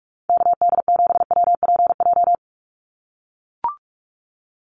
Morse code, heard as a keyed tone, about 44 wpm, 700 Hz. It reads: KD7WPJ
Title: NA SOTA Activator Callsigns 45wpm